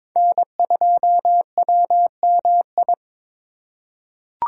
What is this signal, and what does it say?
Morse code, heard as a keyed tone, about 22 wpm, 700 Hz. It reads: N2WMI
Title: USA Call Signs - 5 Characters - 2x Character Spacing - 22wpm